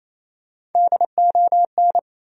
Morse code, heard as a keyed tone, about 28 wpm, 700 Hz. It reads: DON
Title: USA Common Names - Max 3 Letters - 28wpm